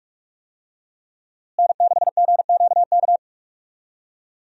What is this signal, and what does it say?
Morse code, heard as a keyed tone, about 46 wpm, 700 Hz. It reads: N6GQK
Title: USA Call Signs - 5 Characters - Repeated 2x - 45wpm